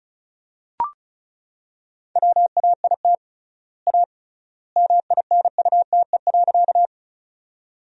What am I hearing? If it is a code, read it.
Morse code, heard as a keyed tone, about 36 wpm, 700 Hz. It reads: WAIT A MINUTE.
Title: 3x Word-Spacing - Sentences from Top 500 Words - 35wpm